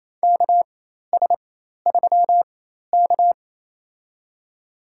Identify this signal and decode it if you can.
Morse code, heard as a keyed tone, about 28 wpm, spaced as for 13 wpm, 700 Hz. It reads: KS3K
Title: USA Call Signs - 4 Characters - 4x Character Spacing - 28wpm